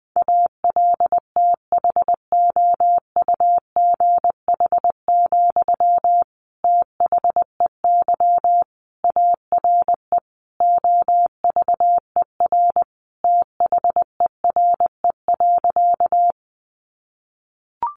Morse code, heard as a keyed tone, about 20 wpm, 700 Hz. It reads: ALTHOUGH, THEY ARE OVER THERE.
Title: Sentences from Top 600 Words 20wpm